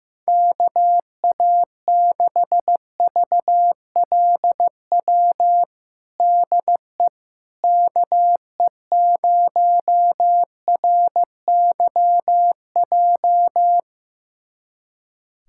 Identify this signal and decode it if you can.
Morse code, heard as a keyed tone, about 16 wpm, 700 Hz. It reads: KA6VLW DE KE0RYJ
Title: USA Callsign DE Callsign - 6 Characters - 15wpm